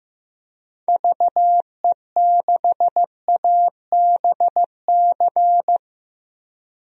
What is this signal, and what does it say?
Morse code, heard as a keyed tone, about 16 wpm, 700 Hz. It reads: VE6ABC
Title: DXCC-100 Call Signs - 6 Characters - 15wpm